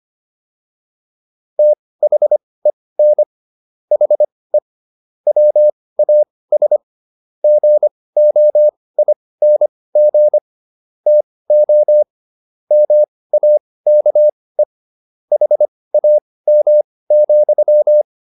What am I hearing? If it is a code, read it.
Morse code, heard as a keyed tone, about 25 wpm, spaced as for 18 wpm, 600 Hz. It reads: THEN HE WAS GOING TO MAKE HAM,